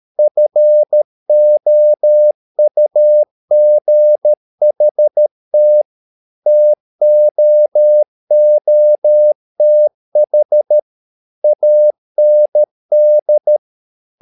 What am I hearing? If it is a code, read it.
Morse code, heard as a keyed tone, about 13 wpm, 600 Hz. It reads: FOUGHT TOOTH AND
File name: mp3 files / Lexiphiles - 13wpm